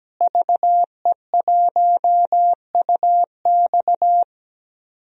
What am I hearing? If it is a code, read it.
Morse code, heard as a keyed tone, about 17 wpm, 700 Hz. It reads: VE1UX
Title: DXCC-100 Call Signs - 5 Characters - Repeated 3x - 17wpm